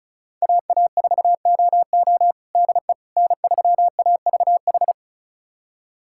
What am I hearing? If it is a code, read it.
Morse code, heard as a keyed tone, about 34 wpm, 700 Hz. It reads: AA4OO DE N3AVH
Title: USA Callsign DE Callsign - 5 Characters - 35wpm